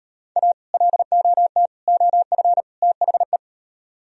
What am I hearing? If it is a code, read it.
Morse code, heard as a keyed tone, about 35 wpm, 700 Hz. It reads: A LOT OF THE